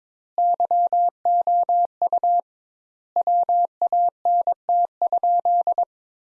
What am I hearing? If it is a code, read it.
Morse code, heard as a keyed tone, about 22 wpm, 700 Hz. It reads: YOU WANT?
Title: Sentences Top 200 Words - 22wpm